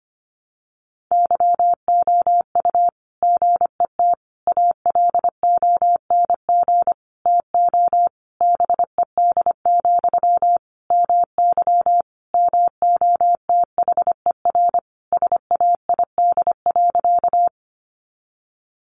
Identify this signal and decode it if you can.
Morse code, heard as a keyed tone, about 25 wpm, 700 Hz. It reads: YOU GET ALONG TO BED, MY MOTHER SAID.
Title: Sentences from Top 700 Words 25wpm